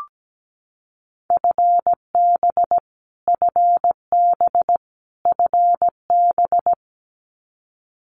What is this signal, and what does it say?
Morse code, heard as a keyed tone, about 17 wpm, 700 Hz. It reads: FB FB FB
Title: Abbreviations Repeated 3x - 17wpm